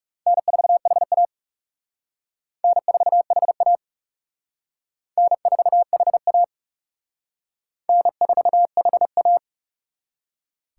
Morse code, heard as a keyed tone, about 40 wpm, 700 Hz. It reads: N4HA N4HA N4HA N4HA